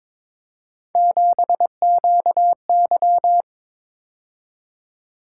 Morse code, heard as a keyed tone, about 26 wpm, 700 Hz. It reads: 7QY
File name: Speed-Racing - Three Letter-Number Combinations - 22wpm